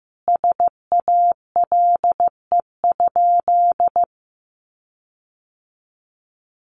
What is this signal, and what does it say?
Morse code, heard as a keyed tone, about 14 wpm, 700 Hz. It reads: SALE?
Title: Sentences from Top 1400 Words 15wpm